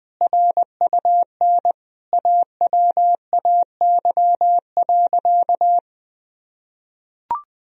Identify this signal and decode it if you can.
Morse code, heard as a keyed tone, about 20 wpm, 700 Hz. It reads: RUN AWAY.
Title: Sentences from Top 1000 Words - Review Part 1 - 20wpm